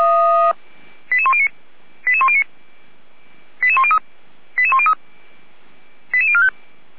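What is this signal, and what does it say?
Five-tone selective call (ZVEI), heard as quick runs of five tones, tones of about 70 ms. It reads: 88109 88109 88103 88103 88045
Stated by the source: ZVEI five-tone selective call 88569 received with an SDR.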